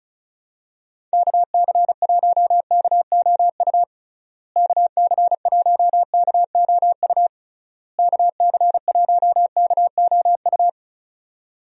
Morse code, heard as a keyed tone, about 34 wpm, 700 Hz. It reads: KC1KOU KC1KOU KC1KOU
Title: USA Call Signs - 6 Characters - Repeated 3x - 35wpm